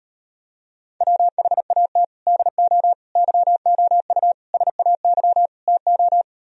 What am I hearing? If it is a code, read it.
Morse code, heard as a keyed tone, about 38 wpm, 700 Hz. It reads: WHAT DO YOU SAY TO